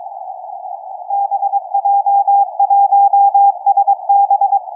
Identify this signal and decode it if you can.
Morse code, heard as a keyed tone, about 22 wpm, 800 Hz, with radio noise. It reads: BJ1SB
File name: XW2A-CAS3A-VERT-02 17h46 utc 145mhz660 cw 2